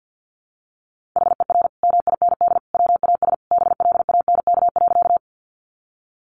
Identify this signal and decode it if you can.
Morse code, heard as a keyed tone, about 50 wpm, 700 Hz. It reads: HER MIND WAS BLANK.
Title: Sentences from Top 1500 Words 50wpm